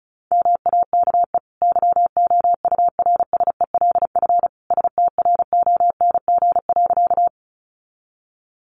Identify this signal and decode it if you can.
Morse code, heard as a keyed tone, about 34 wpm, 700 Hz. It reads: MAKE YOURSELF STRONG.
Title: Sentences from Top 600 Words 35wpm